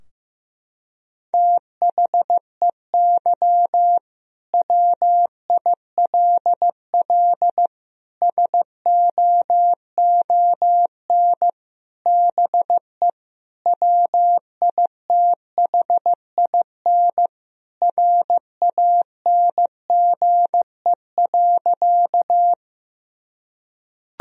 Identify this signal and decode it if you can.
Morse code, heard as a keyed tone, about 15 wpm, 700 Hz. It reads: THEY WILL SOON BE WITHIN RANGE.
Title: Sentences from Top 1000 Words - Review Part 4 - 15wpm